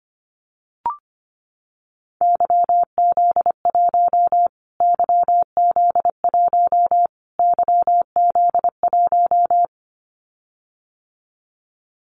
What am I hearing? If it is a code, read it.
Morse code, heard as a keyed tone, about 25 wpm, 700 Hz. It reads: YZ1 YZ1 YZ1
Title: Callsign Prefixes for Top 25 DXCC - Repeated 3x - 25wpm